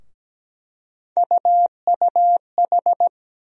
Morse code, heard as a keyed tone, about 17 wpm, 700 Hz. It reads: UUH
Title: USA Call Sign Postfixes - 3 Letters 17wpm